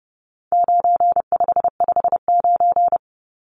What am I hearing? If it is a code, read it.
Morse code, heard as a keyed tone, about 30 wpm, 700 Hz. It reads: 9559
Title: Numbers - 4-Digit 30wpm